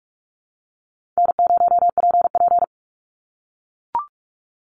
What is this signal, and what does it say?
Morse code, heard as a keyed tone, about 44 wpm, 700 Hz. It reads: N0PP
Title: Easy Call Signs - 45wpm